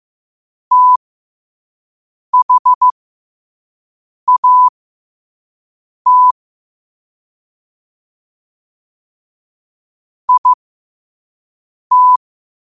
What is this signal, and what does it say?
Morse code, heard as a keyed tone, about 15 wpm, spaced as for 5 wpm, 1000 Hz. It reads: THAT IT
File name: qotdfiles / 5wpm